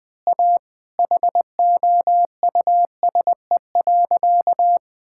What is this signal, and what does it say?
Morse code, heard as a keyed tone, about 20 wpm, 700 Hz. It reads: A HOUSE.
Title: Sentences with 4-Word Phrases - 20wpm